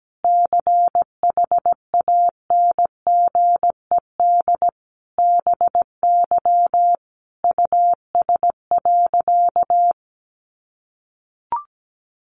Morse code, heard as a keyed tone, about 17 wpm, 700 Hz. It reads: CHANGED BY US.
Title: A Sound Of Thunder 17wpm